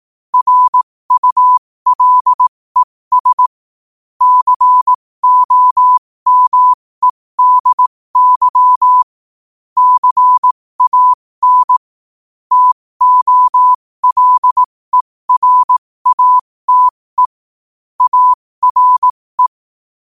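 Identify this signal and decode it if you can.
Morse code, heard as a keyed tone, about 18 wpm, spaced as for 15 wpm, 1000 Hz. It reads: RULES COMEDY CAN TOLERATE ARE
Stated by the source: Morse quote-of-the-day practice at 15 WPM from Morse Resource.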